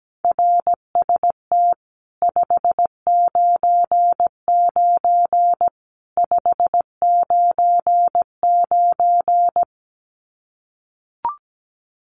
Morse code, heard as a keyed tone, about 17 wpm, 700 Hz. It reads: RST 599 599
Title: Common QSO Phrases 17wpm